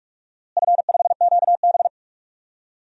Morse code, heard as a keyed tone, about 46 wpm, 700 Hz. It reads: F5QB
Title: DXCC-100 Call Signs - 4 Characters - Repeated 3x - 45wpm